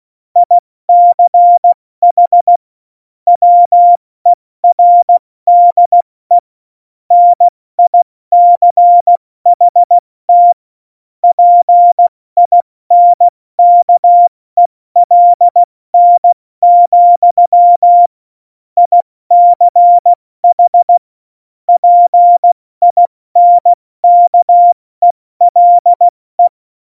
Morse code, heard as a keyed tone, about 16 wpm, 700 Hz. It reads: ICH WERDE NICHT PINKELN, ICH PINKELE
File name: Morsekurs / Witze / Witz 28. November 2017, 70 F80